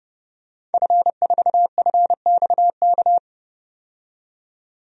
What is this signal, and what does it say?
Morse code, heard as a keyed tone, about 30 wpm, 700 Hz. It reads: F4FXK
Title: DXCC-100 Call Signs - 5 Characters - 30wpm